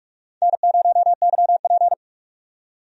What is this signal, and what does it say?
Morse code, heard as a keyed tone, about 46 wpm, 700 Hz. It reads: N0YP
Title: USA Call Signs - 4 Characters - 45wpm